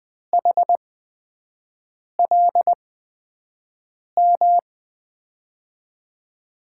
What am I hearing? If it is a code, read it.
Morse code, heard as a keyed tone, about 21 wpm, spaced as for 5 wpm, 700 Hz. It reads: HLM
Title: USA Call Signs - 8x Character Spacing - 20wpm